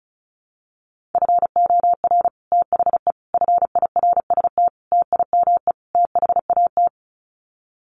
Morse code, heard as a keyed tone, about 34 wpm, 700 Hz. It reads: FOR THE FIRST TIME THAT
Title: Sets of 5 Words Encore 35wpm